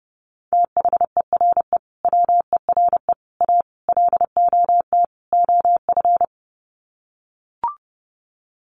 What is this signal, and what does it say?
Morse code, heard as a keyed tone, about 28 wpm, 700 Hz. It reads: THERE WERE A LOT OF
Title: Sets of 5 Words 30wpm